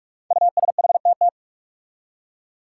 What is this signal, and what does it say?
Morse code, heard as a keyed tone, about 44 wpm, 700 Hz. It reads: USHTT
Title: Five Challenging Letter-Number Combinations 45wpm